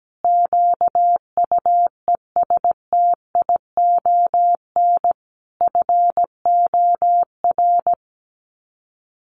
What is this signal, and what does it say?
Morse code, heard as a keyed tone, about 17 wpm, 700 Hz. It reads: QUESTION FOR
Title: Sets of 5 Words Encore2 17wpm